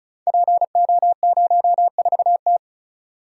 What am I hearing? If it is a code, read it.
Morse code, heard as a keyed tone, about 36 wpm, 700 Hz. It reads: PO04T
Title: Code Groups - 5 Letter:Numbers - 35wpm